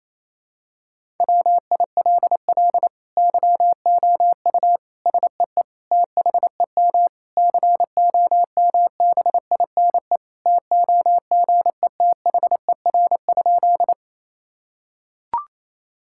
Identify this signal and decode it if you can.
Morse code, heard as a keyed tone, about 28 wpm, 700 Hz. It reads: WILL YOU SEE THEM COMBINE TOGETHER?
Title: Sentences from Top 1600 Words 28wpm